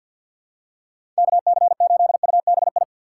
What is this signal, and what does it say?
Morse code, heard as a keyed tone, about 47 wpm, 700 Hz. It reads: KC8UBI
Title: USA Call Signs - 4x Character Spacing - 50wpm